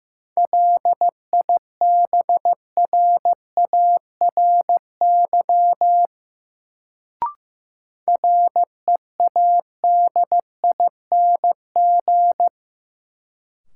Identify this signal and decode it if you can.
Morse code, heard as a keyed tone, about 15 wpm, 700 Hz. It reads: LIBRARY READING
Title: Sentences from Top 600 Words - New Word Drill - 15wpm